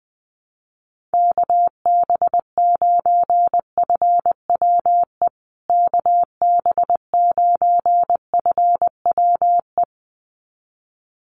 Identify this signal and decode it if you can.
Morse code, heard as a keyed tone, about 20 wpm, 700 Hz. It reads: KB9FWE KB9FWE
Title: USA Call Signs Repeated 2x 20wpm